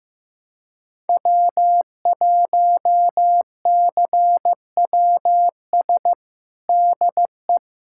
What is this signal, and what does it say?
Morse code, heard as a keyed tone, about 15 wpm, 700 Hz. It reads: W1CWS DE
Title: USA Callsign DE Callsign - 5 Characters - 15wpm